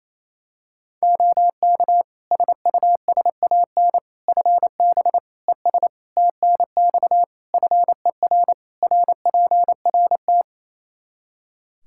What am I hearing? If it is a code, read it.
Morse code, heard as a keyed tone, about 27 wpm, 700 Hz. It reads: OK SUSAN FB ES TNX FER RPRT
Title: Practice QSO - Exchange 2 - Info - 28wpm